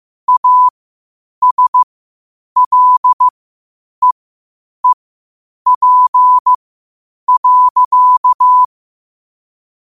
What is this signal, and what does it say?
Morse code, heard as a keyed tone, about 15 wpm, spaced as for 8 wpm, 1000 Hz. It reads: ASLEEP.
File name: qotdfiles / 8wpm